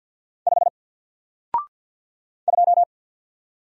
Morse code, heard as a keyed tone, about 48 wpm, 700 Hz. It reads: 5 2
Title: Single Letter-Number Alt 50wpm